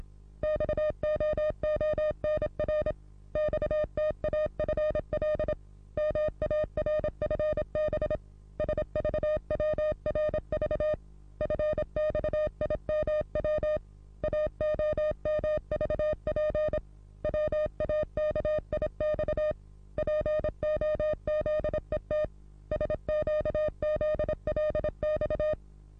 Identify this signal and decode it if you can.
Morse code, heard as a keyed tone, about 26 wpm, 610 Hz. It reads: XOONR XTAFL MARFB SVWRV FXIMW AOMVP WAKIX POZET SQZLX
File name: repos / Morsekurs / DJ4UF Morsen27